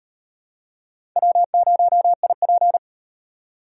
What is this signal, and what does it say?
Morse code, heard as a keyed tone, about 36 wpm, 700 Hz. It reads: W0IP